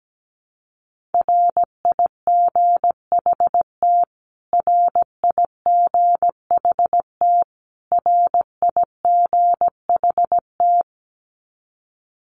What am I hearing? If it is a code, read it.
Morse code, heard as a keyed tone, about 17 wpm, 700 Hz. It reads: RIGHT RIGHT RIGHT
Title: Top 200 Words Repeated 17wpm